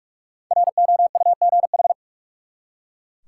Morse code, heard as a keyed tone, about 43 wpm, 700 Hz. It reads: ROUGH
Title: Five Letter Words - 45wpm